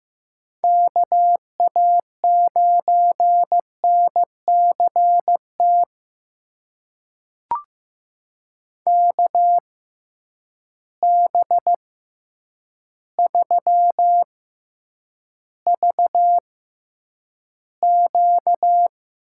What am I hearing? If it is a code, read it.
Morse code, heard as a keyed tone, about 15 wpm, spaced as for 8 wpm, 700 Hz. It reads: KA9NCT KB3VQ
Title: USA Call Signs - 6 Characters - 6x Character Spacing - 15wpm